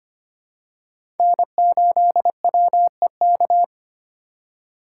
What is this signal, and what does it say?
Morse code, heard as a keyed tone, about 25 wpm, 700 Hz. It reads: N8WEK
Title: USA Call Signs - 5 Characters - 4x Character Spacing - 25wpm